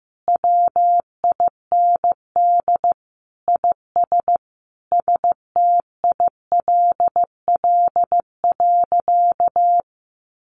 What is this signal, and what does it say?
Morse code, heard as a keyed tone, about 15 wpm, 700 Hz. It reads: WIND IS STILL.
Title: Sentences from Top 1200 Words 15wpm